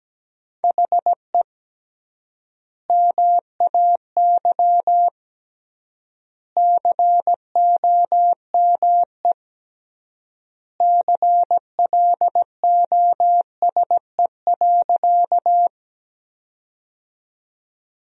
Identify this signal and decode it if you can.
Morse code, heard as a keyed tone, about 17 wpm, 700 Hz. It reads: HE MAY COME CLOSE.
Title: 3x Word-Spacing - Sentences from Top 300 Words - 17wpm